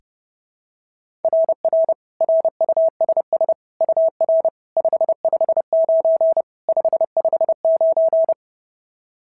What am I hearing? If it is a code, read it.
Morse code, heard as a keyed tone, about 30 wpm, 690 Hz. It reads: RR RUSS UR 559 559
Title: SOTA QSO Practice 30wpm